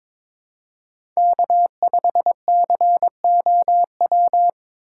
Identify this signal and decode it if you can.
Morse code, heard as a keyed tone, about 22 wpm, 700 Hz. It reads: K5COW
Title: USA Call Signs - 5 Characters - 8x Character Spacing - 22wpm